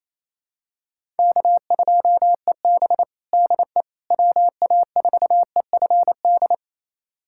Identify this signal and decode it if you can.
Morse code, heard as a keyed tone, about 28 wpm, 700 Hz. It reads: K2EB DE WA4EFD
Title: USA Callsign DE Callsign 28wpm